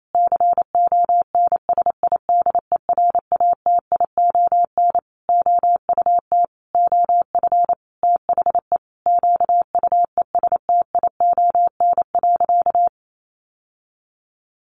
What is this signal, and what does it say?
Morse code, heard as a keyed tone, about 28 wpm, 700 Hz. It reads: CONSIDERATION OUT OF THE QUESTION.
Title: Sentences from Top 1500 Words 28wpm